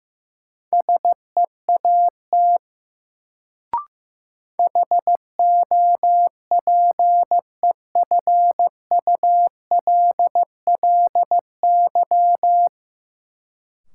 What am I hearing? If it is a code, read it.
Morse code, heard as a keyed tone, about 16 wpm, 700 Hz. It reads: SEAT HOPEFULLY
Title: Sentences from Top 1500 Words - New Word Drill - 15wpm